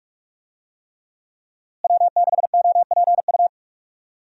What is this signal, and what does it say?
Morse code, heard as a keyed tone, about 46 wpm, 700 Hz. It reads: W6OPU
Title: USA Call Signs - 5 Characters - Repeated 3x - 45wpm